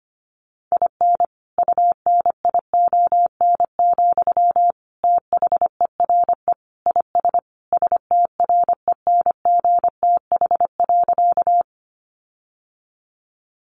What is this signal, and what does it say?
Morse code, heard as a keyed tone, about 25 wpm, 700 Hz. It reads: IN UNION, THERE IS STRENGTH.
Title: Sentences from Top 1400 Words 25wpm